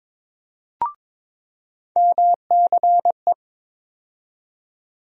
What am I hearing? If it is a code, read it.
Morse code, heard as a keyed tone, about 23 wpm, 700 Hz. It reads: MCE
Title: USA Call Sign Postfixes - 3 Letters 22wpm